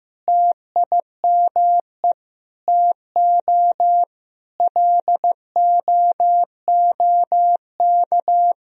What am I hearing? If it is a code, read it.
Morse code, heard as a keyed tone, about 15 wpm, 700 Hz. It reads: TIME TO LOOK
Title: Sentences from Top 1000 Words - Review Part 3 - 15wpm